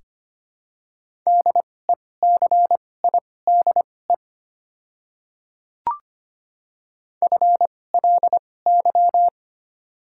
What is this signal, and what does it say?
Morse code, heard as a keyed tone, about 25 wpm, spaced as for 18 wpm, 700 Hz. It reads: DECIDE FLY
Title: 2x Character Spacing - Top 500 Words - 25wpm